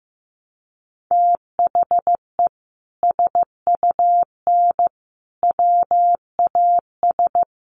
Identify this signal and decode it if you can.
Morse code, heard as a keyed tone, about 15 wpm, 700 Hz. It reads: THE SUN WAS
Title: Sentences from Top 1300 Words 15wpm